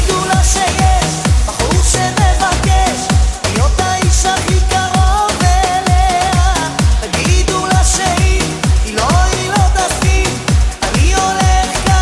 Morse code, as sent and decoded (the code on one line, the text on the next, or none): none